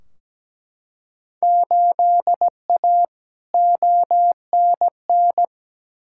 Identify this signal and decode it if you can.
Morse code, heard as a keyed tone, about 17 wpm, 700 Hz. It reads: 8A ONN
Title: Field Day Exchanges Encore - 17wpm